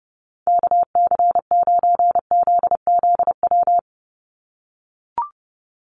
Morse code, heard as a keyed tone, about 30 wpm, 700 Hz. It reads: KC9ZZW
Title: USA Call Signs 30wpm